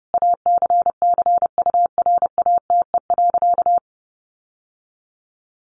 Morse code, heard as a keyed tone, about 30 wpm, 700 Hz. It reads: ACCURATE.
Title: Sentences from Top 1200 Words 30wpm